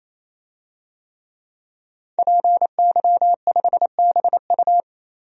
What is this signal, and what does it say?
Morse code, heard as a keyed tone, about 27 wpm, 700 Hz. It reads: PY5BU
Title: DXCC-100 Call Signs - 5 Characters - 28wpm